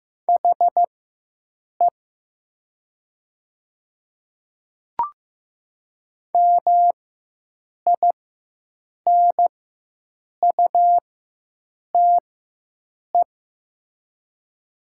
Morse code, heard as a keyed tone, about 15 wpm, spaced as for 7 wpm, 700 Hz. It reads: HE MINUTE
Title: 4x Character Spacing - Top 500 Words - 15wpm